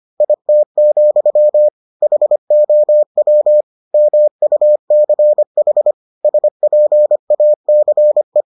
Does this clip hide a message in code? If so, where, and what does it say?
Yes: Morse code from 0.2 to 8.4 s, heard as a keyed tone, about 25 wpm, 600 Hz. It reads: IT, HOW MUCH SPACE